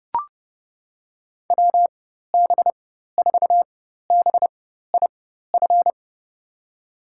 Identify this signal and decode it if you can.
Morse code, heard as a keyed tone, about 30 wpm, spaced as for 14 wpm, 700 Hz. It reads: WB4BIF
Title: USA Call Signs - 6 Characters - 4x Character Spacing - 30wpm